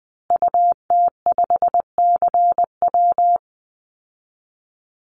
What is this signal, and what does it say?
Morse code, heard as a keyed tone, about 20 wpm, 700 Hz. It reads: UT5CW
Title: Call Signs Repeated 2x - 20wpm